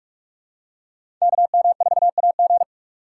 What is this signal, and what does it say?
Morse code, heard as a keyed tone, about 46 wpm, 700 Hz. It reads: KM4AG
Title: USA Call Signs - 3x Character Spacing - 45wpm